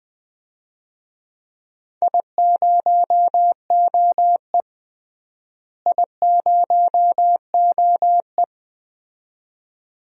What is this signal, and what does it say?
Morse code, heard as a keyed tone, about 21 wpm, 700 Hz. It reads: I0OE I0OE
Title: DXCC-100 Call Signs - 4 Characters - Repeated 2x - 20wpm